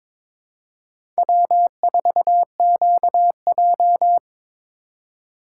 Morse code, heard as a keyed tone, about 22 wpm, 700 Hz. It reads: W4QJ
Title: USA Call Signs - 4x Character Spacing - 22wpm